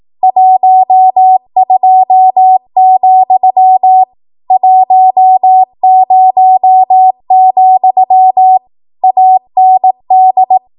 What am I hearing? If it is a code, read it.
Morse code, heard as a keyed tone, about 18 wpm, 750 Hz. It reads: 12, 10, AND
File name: Morse / 18WPM